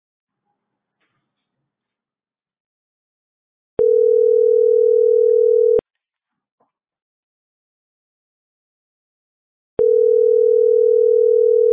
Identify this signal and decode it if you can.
Ringback tone (North America), 440 + 480 Hz.